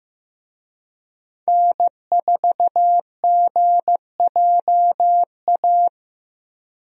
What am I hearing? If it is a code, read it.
Morse code, heard as a keyed tone, about 15 wpm, 700 Hz. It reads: N4GJA
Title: USA Call Signs - 5 Characters - 15wpm